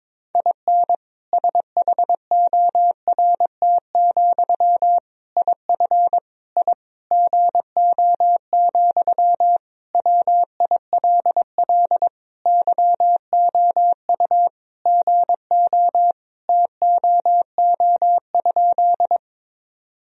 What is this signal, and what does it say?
Morse code, heard as a keyed tone, about 22 wpm, 700 Hz. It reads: IN SHORT, IF I GO, WILL YOU GO TOO?
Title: Sentences from Top 1000 Words - Review Part 2 - 22wpm